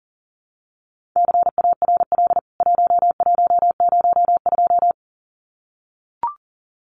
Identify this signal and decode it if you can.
Morse code, heard as a keyed tone, about 40 wpm, 700 Hz. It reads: CARL 1102
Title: CWT Exchange 40wpm